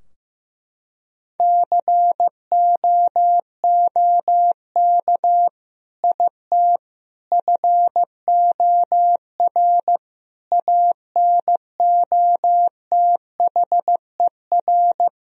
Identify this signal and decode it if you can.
Morse code, heard as a keyed tone, about 15 wpm, 700 Hz. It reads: COOK IT FOR ANOTHER
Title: Sentences from Top 1000 Words - Review Part 4 - 15wpm